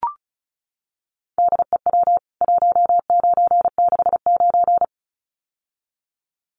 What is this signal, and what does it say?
Morse code, heard as a keyed tone, about 34 wpm, 700 Hz. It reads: DEW 1969
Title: CWT Exchange 35wpm